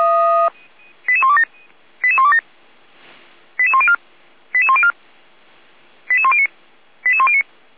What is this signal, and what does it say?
Five-tone selective call (ZVEI), heard as quick runs of five tones, tones of about 70 ms. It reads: 88127 88127 88104 88104 88109 88109